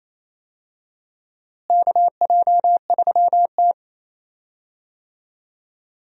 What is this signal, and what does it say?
Morse code, heard as a keyed tone, about 28 wpm, 700 Hz. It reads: KJ3T
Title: USA Call Signs - 4 Characters - Repeated 3x - 28wpm